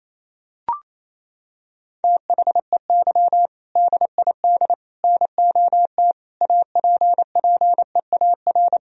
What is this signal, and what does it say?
Morse code, heard as a keyed tone, about 27 wpm, 700 Hz. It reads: THEY DID NOT APPEAR
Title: Sentences from Top 1000 Words - Review Part 2 - 28wpm